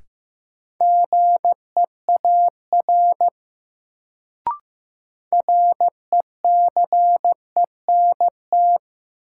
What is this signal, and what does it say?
Morse code, heard as a keyed tone, about 15 wpm, 700 Hz. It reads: GEAR RECENT
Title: Sentences from Top 1400 Words - New Word Drill Alt - 15wpm